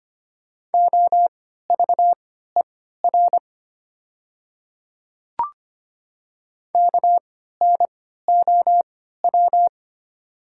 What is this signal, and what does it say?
Morse code, heard as a keyed tone, about 25 wpm, spaced as for 14 wpm, 700 Hz. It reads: OVER KNOW
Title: 3x Character Spacing - Top 100 Words - 25wpm